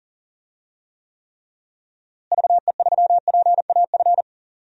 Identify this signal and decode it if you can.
Morse code, heard as a keyed tone, about 40 wpm, 700 Hz. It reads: VE3PAF
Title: DXCC-100 Call Signs - 6 Characters - Repeated 3x - 40wpm